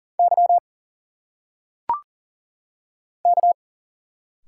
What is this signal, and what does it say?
Morse code, heard as a keyed tone, about 40 wpm, 700 Hz. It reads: Y K
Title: Single Letters - Phonetic Pronunciation 40wpm